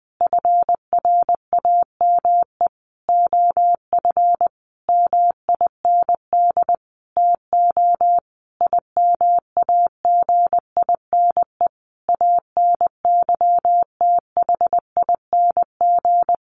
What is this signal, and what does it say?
Morse code, heard as a keyed tone, about 19 wpm, 700 Hz. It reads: FRAME OF MIND TO IMAGINE ANYTHING
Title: Sentences from Top 1400 Words 20wpm